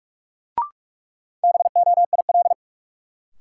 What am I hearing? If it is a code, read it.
Morse code, heard as a keyed tone, about 46 wpm, 700 Hz. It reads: BOIL
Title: Sets of 5 Words Encore - New Word Drill - 45wpm